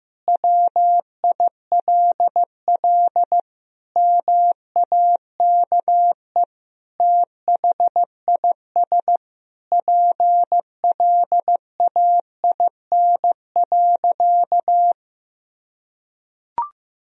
Morse code, heard as a keyed tone, about 15 wpm, 700 Hz. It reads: WILL MAKE THIS PLAIN.
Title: Sentences from Top 1000 Words - Review Part 4 - 15wpm